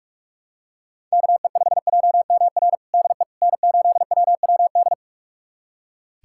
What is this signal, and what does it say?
Morse code, heard as a keyed tone, about 42 wpm, 700 Hz. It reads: KE5JMR DE N8WWD
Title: USA Callsign DE Callsign 45wpm